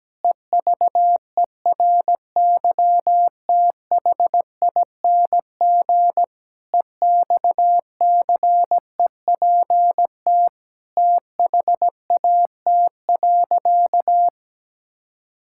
Morse code, heard as a keyed tone, about 17 wpm, 700 Hz. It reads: EVERYTHING EXCEPT THAT.
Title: Sentences from Top 1600 Words 17wpm